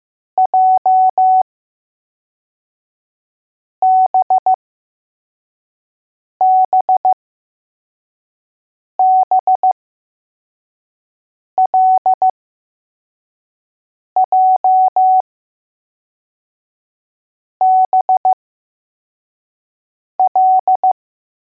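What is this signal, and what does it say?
Morse code, heard as a keyed tone, about 15 wpm, spaced as for 4 wpm, 750 Hz. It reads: JBBBLJBL
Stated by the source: Morse code letters test drill for L, J and B.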